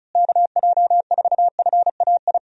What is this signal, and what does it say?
Morse code, heard as a keyed tone, about 34 wpm, 700 Hz. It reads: KJ4FAI
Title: USA Call Signs - 6 Characters - Repeated 3x - 35wpm